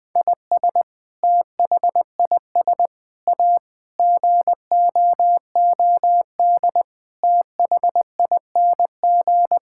Morse code, heard as a keyed tone, about 20 wpm, 700 Hz. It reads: IS THIS A GOOD THING
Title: Sentences with 7-Word Phrases - Spoken Beginning - 20wpm